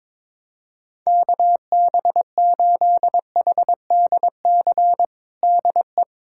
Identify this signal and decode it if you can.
Morse code, heard as a keyed tone, about 22 wpm, 700 Hz. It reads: KB8HDC DE
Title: USA Callsign DE Callsign - 6 Characters - 22wpm